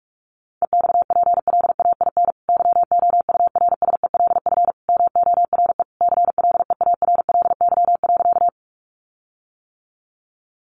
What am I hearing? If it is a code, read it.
Morse code, heard as a keyed tone, about 44 wpm, 700 Hz. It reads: EXPLAIN YOURSELF MORE CLEARLY.
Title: Sentences from Top 1300 Words 45wpm